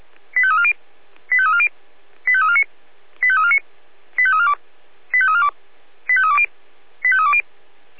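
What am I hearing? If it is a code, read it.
Five-tone selective call (ZVEI), heard as quick runs of five tones, tones of about 70 ms. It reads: 85340 85340 85349 85349 85342 85342 85320 85320
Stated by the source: ZVEI five-tone call 85352 received on an SDR.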